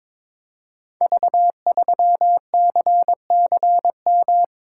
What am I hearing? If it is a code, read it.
Morse code, heard as a keyed tone, about 22 wpm, 700 Hz. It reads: V3CCM
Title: Code Groups - 5 Letter:Numbers - 22wpm